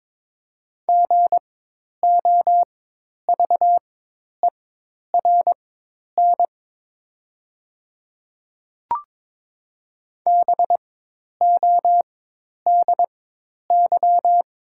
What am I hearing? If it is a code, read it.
Morse code, heard as a keyed tone, about 22 wpm, spaced as for 10 wpm, 700 Hz. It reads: GOVERN BODY
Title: 4x Character Spacing - Top 500 Words - 22wpm